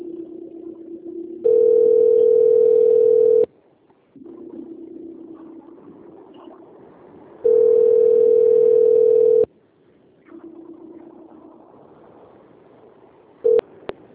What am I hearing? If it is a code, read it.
Ringback tone (North America), 440 + 480 Hz.
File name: out- - - - - .63801